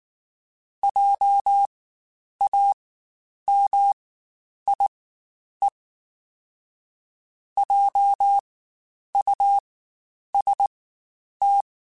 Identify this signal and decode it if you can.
Morse code, heard as a keyed tone, about 19 wpm, spaced as for 9 wpm, 800 Hz. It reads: JAMIE JUST
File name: Spaced Alphas / Spaced Alphas mp3s / Js sp @18